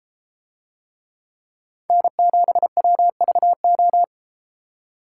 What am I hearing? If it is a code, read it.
Morse code, heard as a keyed tone, about 32 wpm, 700 Hz. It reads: N7WVO